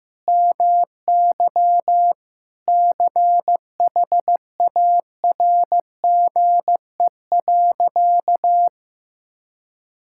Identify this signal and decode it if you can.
Morse code, heard as a keyed tone, about 15 wpm, 700 Hz. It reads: MY CHARGE.
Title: Sentences from Top 1000 Words - Review Part 6 - 15wpm